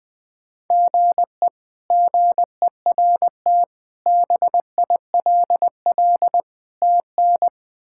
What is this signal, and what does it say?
Morse code, heard as a keyed tone, about 21 wpm, 700 Hz. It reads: GE GERT BILL TN
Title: K1USN SST QSO Practice - 20wpm